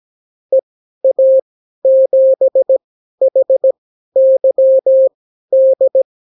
Morse code, heard as a keyed tone, about 17 wpm, spaced as for 12 wpm, 530 Hz. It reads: EA7HYD
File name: CW QSO libroCW4 18-8ppm